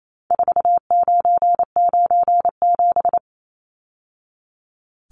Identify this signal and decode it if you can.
Morse code, heard as a keyed tone, about 28 wpm, 700 Hz. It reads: 4997
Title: Numbers - 4-Digit 28wpm